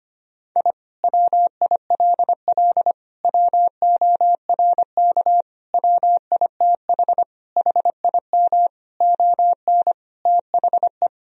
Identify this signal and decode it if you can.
Morse code, heard as a keyed tone, about 25 wpm, 700 Hz. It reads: I WILL WORK WITH HIM ON THE
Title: Sentences from Top 1000 Words - Review Part 4 - 25wpm